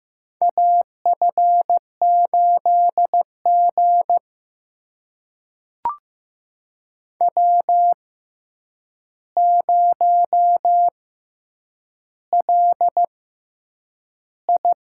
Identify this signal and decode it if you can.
Morse code, heard as a keyed tone, about 15 wpm, spaced as for 8 wpm, 700 Hz. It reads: AF8G W0LI
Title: USA Call Signs - 4 Characters - 6x Character Spacing - 15wpm